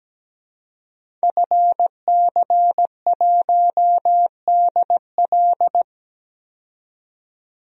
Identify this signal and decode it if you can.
Morse code, heard as a keyed tone, about 17 wpm, 700 Hz. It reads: FC1DL
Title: DXCC-100 Call Signs - 5 Characters - 17wpm